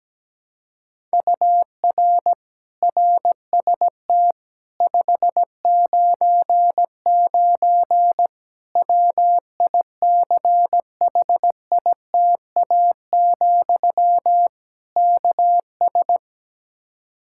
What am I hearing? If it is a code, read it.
Morse code, heard as a keyed tone, about 17 wpm, 700 Hz. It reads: UR RST 599 WICHITA, KS
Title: Practice QSO - Exchange 1 - Beginning - 17wpm